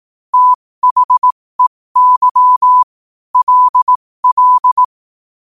Morse code, heard as a keyed tone, about 18 wpm, spaced as for 15 wpm, 1000 Hz. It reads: THEY LL
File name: 15wpm